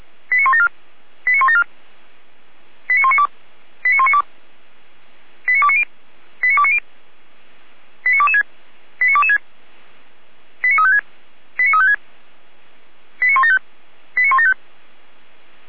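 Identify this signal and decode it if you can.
Five-tone selective call (ZVEI), heard as quick runs of five tones, tones of about 70 ms. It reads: 89174 89174 89192 89192 89290 89290 89226 89226 89356 89356 89175 89175
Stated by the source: Five-tone ZVEI selective call received by SDR.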